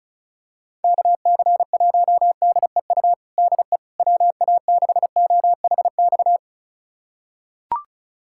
Morse code, heard as a keyed tone, about 36 wpm, 700 Hz. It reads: KC1DEU DE WA6OHX
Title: USA Callsign DE Callsign - 6 Characters - 35wpm